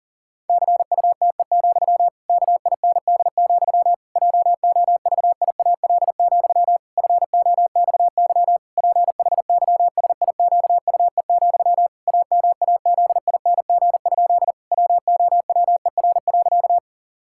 Morse code, heard as a keyed tone, about 37 wpm, 700 Hz. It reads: CUTE, KIND, JOVIAL, FOXY PHYSIQUE, AMAZING? WOWER.
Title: Pangram Challenge v2 - 40wpm